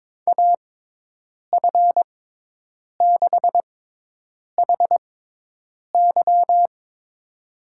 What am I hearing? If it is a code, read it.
Morse code, heard as a keyed tone, about 22 wpm, spaced as for 8 wpm, 700 Hz. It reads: AF6HY
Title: USA Call Signs - 5 Characters - 6x Character Spacing - 22wpm